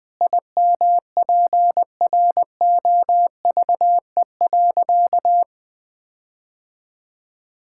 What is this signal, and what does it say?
Morse code, heard as a keyed tone, about 20 wpm, 700 Hz. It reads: IMPROVE.
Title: Sentences with 4-Word Phrases - 20wpm